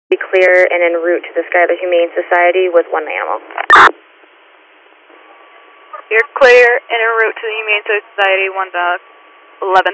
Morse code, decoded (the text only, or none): none